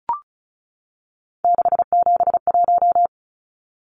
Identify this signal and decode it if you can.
Morse code, heard as a keyed tone, about 34 wpm, 700 Hz. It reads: 671
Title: Numbers - 3-Digit 35wpm